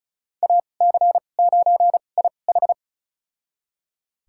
Morse code, heard as a keyed tone, about 36 wpm, spaced as for 25 wpm, 700 Hz. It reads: AC9IH
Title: USA Call Signs - 2x Character Spacing - 35wpm